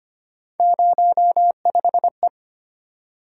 Morse code, heard as a keyed tone, about 25 wpm, 700 Hz. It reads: 05E
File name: Three Letter-Number Combinations - 25wpm